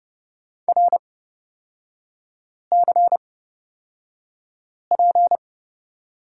Morse code, heard as a keyed tone, about 30 wpm, spaced as for 5 wpm, 700 Hz. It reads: RCP
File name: Lesson 37e - Letter x - Three Characters 30wpm